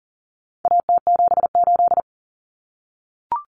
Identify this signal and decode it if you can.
Morse code, heard as a keyed tone, about 38 wpm, 700 Hz. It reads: AT78
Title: Common Cut Numbers - 4-Digit 40wpm